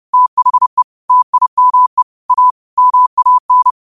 Morse code, heard as a keyed tone, about 30 wpm, 1000 Hz. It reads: THE TIME A MAN
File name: qotdfiles / 30wpm